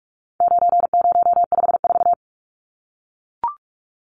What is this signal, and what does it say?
Morse code, heard as a keyed tone, about 44 wpm, 700 Hz. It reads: 9054
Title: Numbers - 4-Digit 45wpm